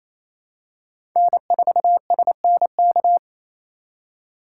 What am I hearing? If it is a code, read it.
Morse code, heard as a keyed tone, about 28 wpm, 700 Hz. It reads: N4SNK
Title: USA Call Signs - 3x Character Spacing - 28wpm